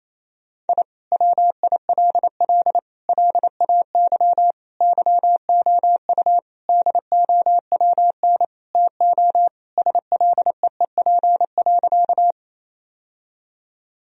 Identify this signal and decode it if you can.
Morse code, heard as a keyed tone, about 28 wpm, 700 Hz. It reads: I WILL LAY YOU DOWN TO SLEEP.
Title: Sentences from Top 1000 Words - Review Part 2 - 28wpm